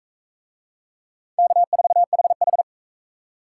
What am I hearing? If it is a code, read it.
Morse code, heard as a keyed tone, about 42 wpm, 700 Hz. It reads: K4HH